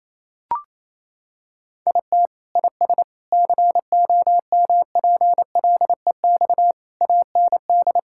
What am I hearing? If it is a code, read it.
Morse code, heard as a keyed tone, about 27 wpm, 700 Hz. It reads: IT IS COMPLEX AND
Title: Sentences from Top 1000 Words - Review Part 6 - 28wpm